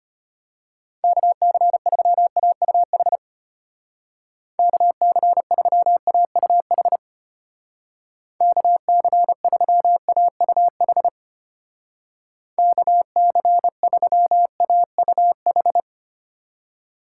Morse code, heard as a keyed tone, about 31 wpm, 700 Hz. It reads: KC3AUH KC3AUH KC3AUH KC3AUH
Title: Speed Racing - USA 2x3 Call Signs 25wpm